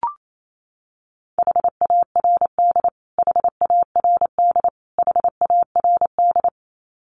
Morse code, heard as a keyed tone, about 28 wpm, 700 Hz. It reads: HARD HARD HARD
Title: Top 500 Words Repeated 28wpm